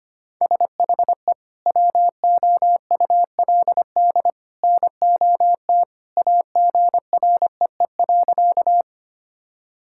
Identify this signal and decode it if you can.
Morse code, heard as a keyed tone, about 25 wpm, 700 Hz. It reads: SHE WOULD NOT AGREE.
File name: practice / 25wpm / Sentences with 2-Word Phrases - 25wpm